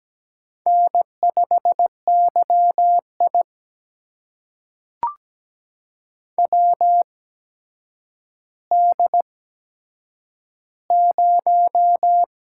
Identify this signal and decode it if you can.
Morse code, heard as a keyed tone, about 17 wpm, spaced as for 8 wpm, 700 Hz. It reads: N5YI WD0
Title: USA Call Signs - 4 Characters - 8x Character Spacing - 17wpm